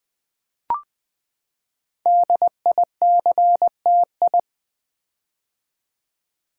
Morse code, heard as a keyed tone, about 20 wpm, 700 Hz. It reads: DICTI
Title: Build A Word - 5 Letters At A Time - 10-Letter Words - 20wpm